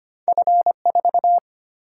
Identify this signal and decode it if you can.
Morse code, heard as a keyed tone, about 25 wpm, 700 Hz. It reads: F4
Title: Single-Letter Single-Number 25wpm